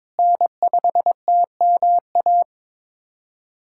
Morse code, heard as a keyed tone, about 21 wpm, 700 Hz. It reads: N5TMA
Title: USA Callsign DE Callsign - 5 Characters - 22wpm